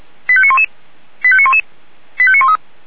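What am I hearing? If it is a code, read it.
Five-tone selective call (ZVEI), heard as quick runs of five tones, tones of about 70 ms. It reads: 85911 85911 85913
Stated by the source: ZVEI five-tone selective call received by SDR.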